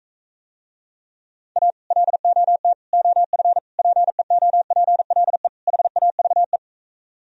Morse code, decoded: A LOT OF PEOPLE HAVE

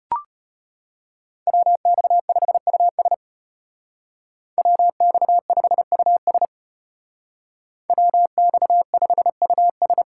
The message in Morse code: WX5US WX5US WX5US

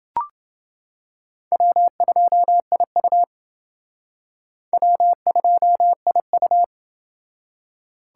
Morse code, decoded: W2IU W2IU

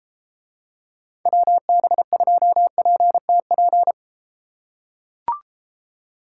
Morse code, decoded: WB2PTP

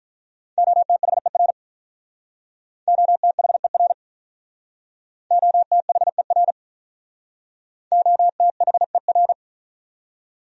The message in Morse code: OTHER OTHER OTHER OTHER

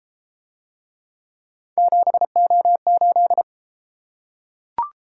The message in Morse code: 7O8

